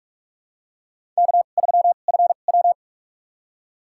Morse code, heard as a keyed tone, about 41 wpm, spaced as for 31 wpm, 690 Hz. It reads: K3FW
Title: USA Call Signs - 4 Characters - 2x Character Spacing - 45wpm